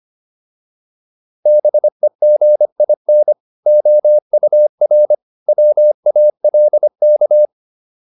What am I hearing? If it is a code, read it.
Morse code, heard as a keyed tone, about 25 wpm, 600 Hz. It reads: BEGIN OUR WALK